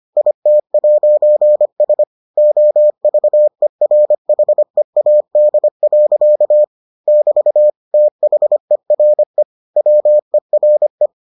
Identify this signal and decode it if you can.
Morse code, heard as a keyed tone, about 25 wpm, 600 Hz. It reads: IT'S OVERHEAD. <BT> THERE WERE